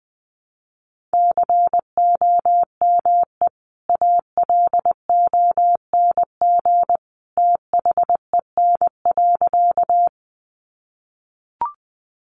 Morse code, heard as a keyed tone, about 19 wpm, 700 Hz. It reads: COME ALONG THEN.
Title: Sentences from Top 700 Words 20wpm